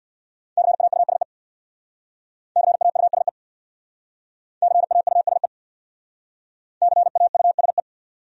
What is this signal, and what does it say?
Morse code, heard as a keyed tone, about 60 wpm, 700 Hz. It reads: CAUSE CAUSE CAUSE CAUSE
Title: Speed-Racing Top 500 Words 50wpm